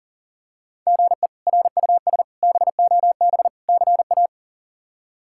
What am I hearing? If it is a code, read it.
Morse code, heard as a keyed tone, about 40 wpm, 700 Hz. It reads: GE RUS BOB CA
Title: K1USN SST QSO Practice - 40wpm